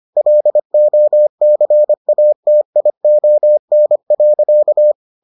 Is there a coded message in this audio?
Yes: Morse code from 0.2 to 4.9 s, heard as a keyed tone, about 25 wpm, 600 Hz. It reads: LOCATION.